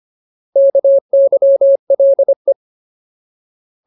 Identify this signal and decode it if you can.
Morse code, heard as a keyed tone, about 25 wpm, 550 Hz. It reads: KYLE
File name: Ultimate Word List / 25WPM Ultimate